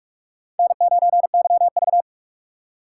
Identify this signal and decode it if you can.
Morse code, heard as a keyed tone, about 46 wpm, 700 Hz. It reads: N9YV
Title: USA Call Signs - 4 Characters - 2x Character Spacing - 45wpm